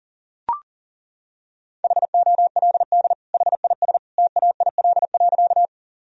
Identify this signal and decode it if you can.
Morse code, heard as a keyed tone, about 40 wpm, 700 Hz. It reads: HOLD HIS TAIL.
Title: Sentences from Top 1000 Words - Review Part 3 - 40wpm